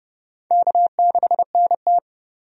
Morse code, heard as a keyed tone, about 32 wpm, 700 Hz. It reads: K6NT
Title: USA Call Signs - 4 Characters - 30wpm